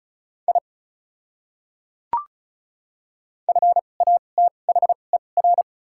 Morse code, heard as a keyed tone, about 34 wpm, spaced as for 25 wpm, 700 Hz. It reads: I FATHER
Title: 2x Character Spacing - Top 300 Words - 35wpm